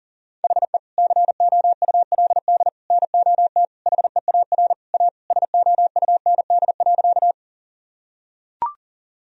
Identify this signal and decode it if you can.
Morse code, heard as a keyed tone, about 40 wpm, 700 Hz. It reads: HE COULD NOT HEAR A SOUND.
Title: Sentences from Top 1000 Words - Review Part 1 - 40wpm